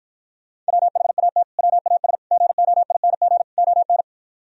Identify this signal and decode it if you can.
Morse code, heard as a keyed tone, about 57 wpm, 700 Hz. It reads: WHAT WAS GOING ON